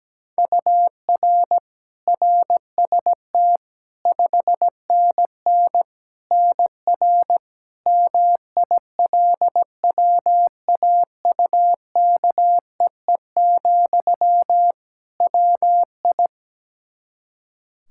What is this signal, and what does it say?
Morse code, heard as a keyed tone, about 17 wpm, 700 Hz. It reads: UR RST 5NN NR MILWAUKEE, WI
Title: Practice QSO - Exchange 1 - Beginning - 17wpm